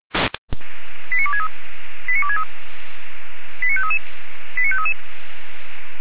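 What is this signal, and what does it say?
Five-tone selective call (ZVEI), heard as quick runs of five tones, tones of about 70 ms. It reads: 89263 89263 89533 89533
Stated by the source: ZVEI five-tone call received by SDR.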